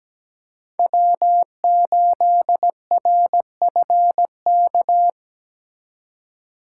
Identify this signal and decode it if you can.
Morse code, heard as a keyed tone, about 20 wpm, 700 Hz. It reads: W8RFK
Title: Speed Racing - USA 1x3 Call Signs 17wpm